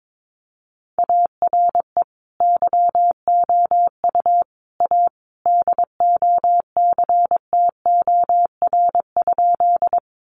Morse code, heard as a keyed tone, about 22 wpm, 700 Hz. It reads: ARE YOU A DOCTOR?
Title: Sentences from Top 1500 Words 22wpm